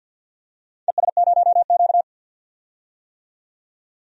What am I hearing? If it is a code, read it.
Morse code, heard as a keyed tone, about 47 wpm, 700 Hz. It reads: ES0Q
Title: DXCC-100 Call Signs - 4 Characters - 50wpm